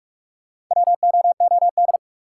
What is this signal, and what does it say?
Morse code, heard as a keyed tone, about 46 wpm, 700 Hz. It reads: WOOD
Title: Sentences from Top 1000 Words - Review Part 2 - New Word Drill - 45wpm